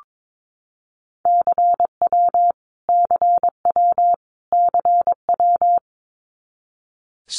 Morse code, heard as a keyed tone, about 22 wpm, 700 Hz. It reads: CW CW CW
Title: Abbreviations Repeated 3x - 22wpm